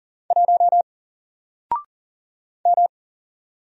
Morse code, heard as a keyed tone, about 40 wpm, 700 Hz. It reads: 1 M
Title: Single Character (Letter:Number:Punctuation:Prosign) 40wpm